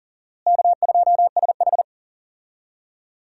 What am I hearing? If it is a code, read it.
Morse code, heard as a keyed tone, about 40 wpm, 700 Hz. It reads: K2SH
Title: USA Call Signs - 4 Characters - Repeated 3x - 40wpm